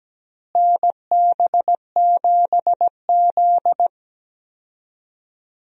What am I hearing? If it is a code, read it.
Morse code, heard as a keyed tone, about 17 wpm, 700 Hz. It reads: NB7Z
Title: USA Call Signs - 4 Characters - Repeated 3x - 17wpm